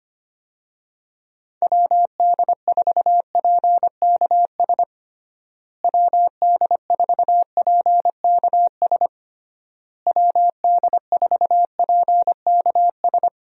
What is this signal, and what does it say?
Morse code, heard as a keyed tone, about 25 wpm, 700 Hz. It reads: WD4PKS WD4PKS WD4PKS
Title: USA Call Signs - 6 Characters - Repeated 3x - 25wpm